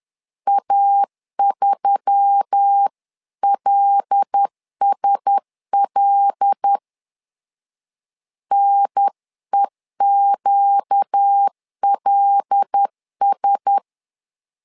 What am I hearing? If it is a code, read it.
Morse code, heard as a keyed tone, about 10 wpm, 800 Hz. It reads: A3LSL NEQLS
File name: LEZIONI / 5 LBVC / 5 L 10wpm 2259